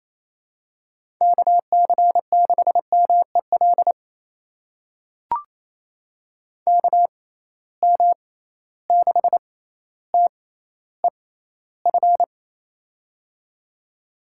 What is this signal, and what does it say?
Morse code, heard as a keyed tone, about 28 wpm, spaced as for 14 wpm, 700 Hz. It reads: KC6MEL KM6TEF
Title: USA Call Signs - 6 Characters - 6x Character Spacing - 28wpm